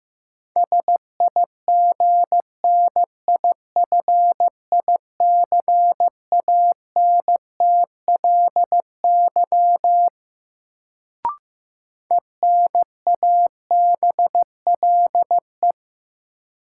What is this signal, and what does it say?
Morse code, heard as a keyed tone, about 15 wpm, 700 Hz. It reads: SIGNIFICANTLY ENABLE
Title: Sentences from Top 1500 Words - New Word Drill Alt - 15wpm